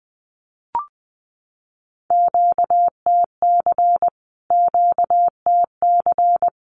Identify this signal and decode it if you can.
Morse code, heard as a keyed tone, about 20 wpm, 700 Hz. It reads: QTC QTC
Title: Q-Codes Repeated 3x 20wpm